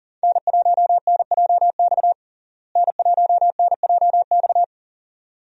Morse code, heard as a keyed tone, about 40 wpm, 700 Hz. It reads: N1NJX N1NJX
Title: USA Call Signs - 5 Characters - Repeated 2x - 40wpm